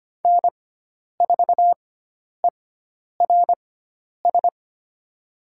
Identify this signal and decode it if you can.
Morse code, heard as a keyed tone, about 25 wpm, spaced as for 10 wpm, 700 Hz. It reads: N4ERS
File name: Lesson 16g - Letter c - Call Signs 25wpm